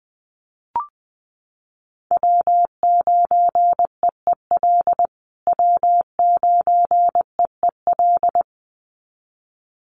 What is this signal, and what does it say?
Morse code, heard as a keyed tone, about 20 wpm, 700 Hz. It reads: W9EEL W9EEL
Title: USA Call Signs Repeated 2x 20wpm